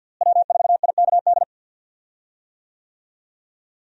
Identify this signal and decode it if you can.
Morse code, heard as a keyed tone, about 51 wpm, 700 Hz. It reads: W4IKD
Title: USA Call Signs - 5 Characters - Repeated 2x - 50wpm